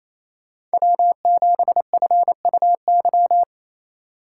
Morse code, heard as a keyed tone, about 37 wpm, 700 Hz. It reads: W7FUY